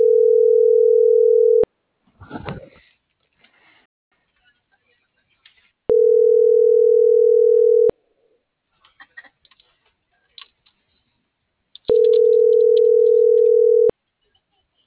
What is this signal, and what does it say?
Ringback tone (North America), 440 + 480 Hz.